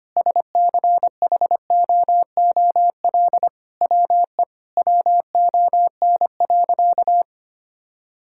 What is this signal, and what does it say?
Morse code, heard as a keyed tone, about 25 wpm, 700 Hz. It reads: SCHOOL WE WON.
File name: practice / 25wpm / Sentences with 9-Word Phrases - 25wpm